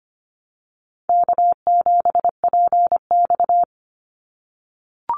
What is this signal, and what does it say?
Morse code, heard as a keyed tone, about 25 wpm, 700 Hz. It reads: K7PX
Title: NA SOTA Activator Callsigns 25wpm